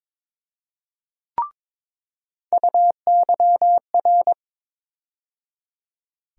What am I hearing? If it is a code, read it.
Morse code, heard as a keyed tone, about 22 wpm, 700 Hz. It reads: UYR
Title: USA Call Sign Postfixes - 3 Letters 22wpm